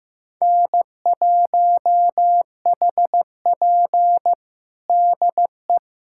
Morse code, heard as a keyed tone, about 16 wpm, 700 Hz. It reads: N1HP DE
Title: USA Callsign DE Callsign - 4 Characters - 15wpm